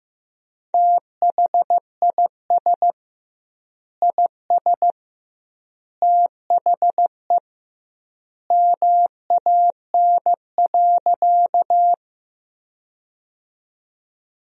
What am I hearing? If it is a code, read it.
Morse code, heard as a keyed tone, about 15 wpm, 700 Hz. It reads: THIS IS THE MAN.
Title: 2x Word-Spacing - Sentences from Top 200 Words - 15wpm